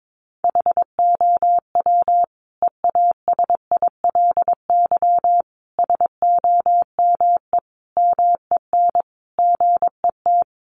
Morse code, heard as a keyed tone, about 22 wpm, 700 Hz. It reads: HOW EASILY SOME MEN GET
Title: Sentences from Top 900 Words 22wpm